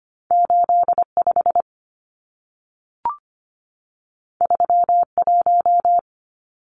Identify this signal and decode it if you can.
Morse code, heard as a keyed tone, about 25 wpm, 700 Hz. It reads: 85 31
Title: Numbers - 2-Digit 25wpm